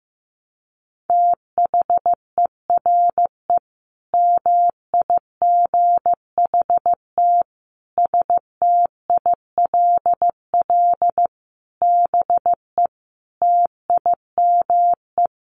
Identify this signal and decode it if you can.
Morse code, heard as a keyed tone, about 14 wpm, 700 Hz. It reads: THERE MIGHT STILL BE TIME
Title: Sentences from Top 1100 Words 15wpm